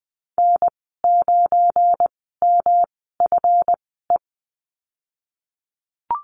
Morse code, heard as a keyed tone, about 20 wpm, spaced as for 15 wpm, 700 Hz. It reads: N9MFE
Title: 238 Lesson 38g - Letter q - Call Signs 20wpm